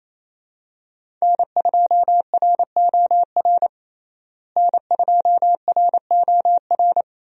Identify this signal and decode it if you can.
Morse code, heard as a keyed tone, about 28 wpm, 700 Hz. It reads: N2ROR N2ROR
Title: USA Call Signs - 5 Characters - Repeated 2x - 28wpm